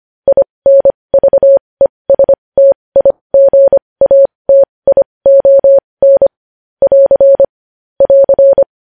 Morse code, heard as a keyed tone, about 25 wpm, spaced as for 20 wpm, 550 Hz. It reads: INVESTIGATION <AR> <AR>
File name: podcast / News headlines-1x-25x20